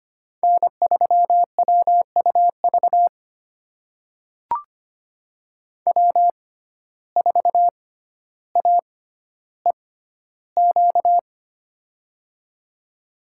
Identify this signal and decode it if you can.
Morse code, heard as a keyed tone, about 25 wpm, spaced as for 13 wpm, 700 Hz. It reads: N3WUV W4AEQ
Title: USA Call Signs - 5 Characters - 6x Character Spacing - 25wpm